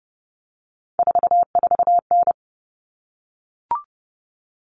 Morse code, heard as a keyed tone, about 30 wpm, 700 Hz. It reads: 44N
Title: Common Cut Numbers - 3-Digit 30wpm